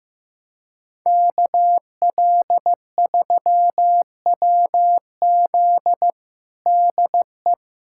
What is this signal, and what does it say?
Morse code, heard as a keyed tone, about 16 wpm, 700 Hz. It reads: KL3WZ DE
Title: USA Callsign DE Callsign - 5 Characters - 15wpm